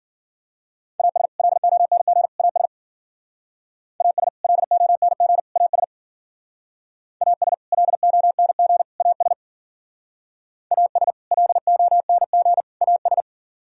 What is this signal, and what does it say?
Morse code, heard as a keyed tone, about 50 wpm, 700 Hz. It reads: AS LONG AS AS LONG AS AS LONG AS AS LONG AS
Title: Speed-Racing - Sets of 3 Words 40wpm